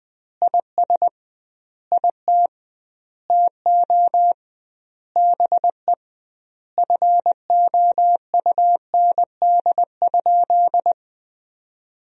Morse code, heard as a keyed tone, about 21 wpm, 700 Hz. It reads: IS IT TO BE FOUND?
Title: 2x Word-Spacing - Sentences from Top 300 Words - 20wpm